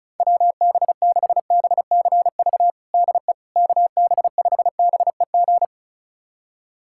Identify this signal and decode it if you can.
Morse code, heard as a keyed tone, about 36 wpm, 700 Hz. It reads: WB6BCV DE KB5BEG
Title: USA Callsign DE Callsign - 6 Characters - 35wpm